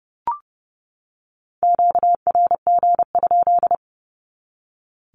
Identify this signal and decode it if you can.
Morse code, heard as a keyed tone, about 30 wpm, 700 Hz. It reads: QRG?
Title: Q-Codes 30wpm